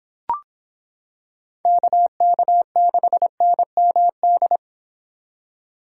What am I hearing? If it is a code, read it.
Morse code, heard as a keyed tone, about 26 wpm, 700 Hz. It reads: KK6NMD